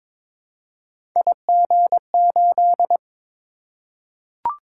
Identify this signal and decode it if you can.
Morse code, heard as a keyed tone, about 21 wpm, 700 Hz. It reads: IG8
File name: practice / 22wpm / Three Letter-Number Combinations - 22wpm